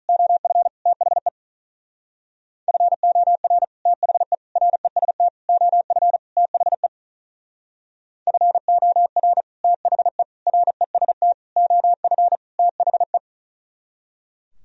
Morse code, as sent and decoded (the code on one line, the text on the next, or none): --- ..-. / - .... / ..-. --- .-. / - .... / .-. ... - / --- ..-. / - .... / ..-. --- .-. / - .... / .-. ... - / --- ..-. / - .... .
OF THE FOR THE REST OF THE FOR THE REST OF THE